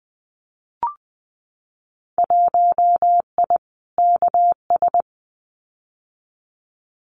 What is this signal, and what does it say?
Morse code, heard as a keyed tone, about 20 wpm, 700 Hz. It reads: 1I KS
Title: Winter Field Day Exchanges Encore 20wpm